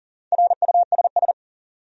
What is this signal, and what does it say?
Morse code, heard as a keyed tone, about 40 wpm, 700 Hz. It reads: RUSS
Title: USA Common Names - Max 5 Letters - 40wpm